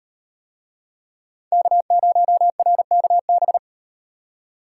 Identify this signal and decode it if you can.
Morse code, heard as a keyed tone, about 38 wpm, 700 Hz. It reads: K0RKB